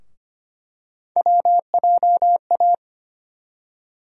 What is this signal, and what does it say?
Morse code, heard as a keyed tone, about 25 wpm, 700 Hz. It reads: WJA
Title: USA Call Sign Postfixes - 3 Letters 25wpm